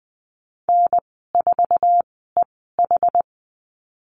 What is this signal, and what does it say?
Morse code, heard as a keyed tone, about 20 wpm, spaced as for 15 wpm, 700 Hz. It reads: N4EH
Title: 088 Lesson 15g - Number 5 - Call Signs 20wpm